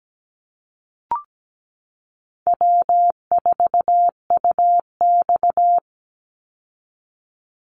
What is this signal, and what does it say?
Morse code, heard as a keyed tone, about 17 wpm, 700 Hz. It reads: W4UX
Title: Easy Call Signs - 17wpm